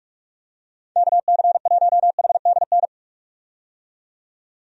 Morse code, heard as a keyed tone, about 46 wpm, 700 Hz. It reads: KC1HDN
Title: USA Call Signs - 6 Characters - 45wpm